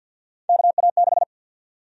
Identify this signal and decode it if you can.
Morse code, heard as a keyed tone, about 53 wpm, 700 Hz. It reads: KA6
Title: USA Call Sign Prefixes - Repeated 2x - 50wpm